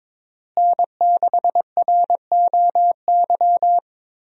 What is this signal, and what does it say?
Morse code, heard as a keyed tone, about 22 wpm, 700 Hz. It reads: N6ROY
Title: USA Call Signs - 5 Characters - Repeated 3x - 22wpm